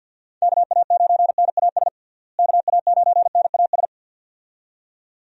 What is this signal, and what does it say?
Morse code, heard as a keyed tone, about 46 wpm, 700 Hz. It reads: KA9NAS KA9NAS
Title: USA Call Signs - 6 Characters - Repeated 2x - 50wpm